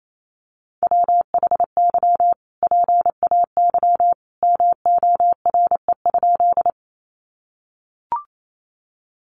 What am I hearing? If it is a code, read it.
Morse code, heard as a keyed tone, about 28 wpm, 700 Hz. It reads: WHY PAY MORE?
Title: Sentences from Top 700 Words 28wpm